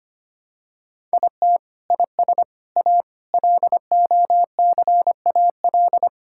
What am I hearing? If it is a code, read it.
Morse code, heard as a keyed tone, about 25 wpm, 700 Hz. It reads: IT IS A LOCAL
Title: Sentences from Top 1000 Words - Review Part 5 - 25wpm